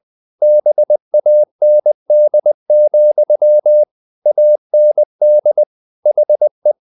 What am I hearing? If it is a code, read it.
Morse code, heard as a keyed tone, about 20 wpm, 600 Hz. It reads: BAND, AND HE
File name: the old man and the sea part 2 20wpm